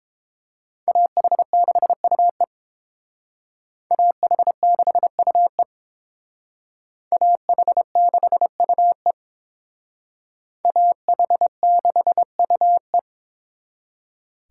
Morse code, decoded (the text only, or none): AH6UE AH6UE AH6UE AH6UE